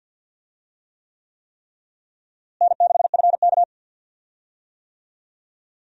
Morse code, heard as a keyed tone, about 46 wpm, 700 Hz. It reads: N6FK
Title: USA Call Signs - 4 Characters - Repeated 3x - 50wpm